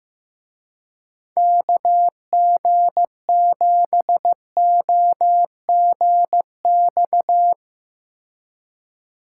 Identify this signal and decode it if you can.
Morse code, heard as a keyed tone, about 16 wpm, 700 Hz. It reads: KG7OGX
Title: Speed Racing - USA 2x3 Call Signs 15wpm